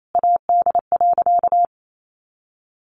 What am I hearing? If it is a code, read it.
Morse code, heard as a keyed tone, about 28 wpm, 700 Hz. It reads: AD.
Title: Sentences from Top 1300 Words 28wpm